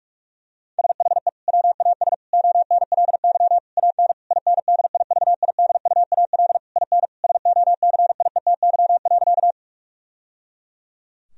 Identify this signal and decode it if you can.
Morse code, heard as a keyed tone, about 46 wpm, 700 Hz. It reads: SHE WAS ONLY AN INDIVIDUAL IN SOCIETY.
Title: Sentences from Top 1000 Words - Review Part 4 - 45wpm